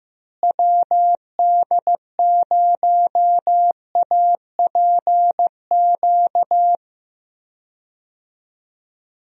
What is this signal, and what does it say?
Morse code, heard as a keyed tone, about 16 wpm, 700 Hz. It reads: WD0APQ
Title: USA Call Signs - 6 Characters - Repeated 3x - 15wpm